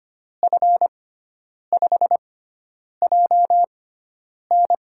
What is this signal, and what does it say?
Morse code, heard as a keyed tone, about 25 wpm, spaced as for 9 wpm, 700 Hz. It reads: F5JN
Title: USA Call Signs - 6x Character Spacing - 25wpm